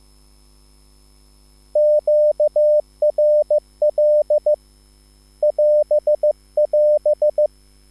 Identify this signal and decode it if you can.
Morse code, heard as a keyed tone, about 15 wpm, 600 Hz. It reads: QRL <AS><AS>
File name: QSL / CW lesson